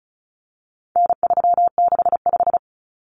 Morse code, heard as a keyed tone, about 34 wpm, 700 Hz. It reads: N365
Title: Common Cut Numbers - 4-Digit 35wpm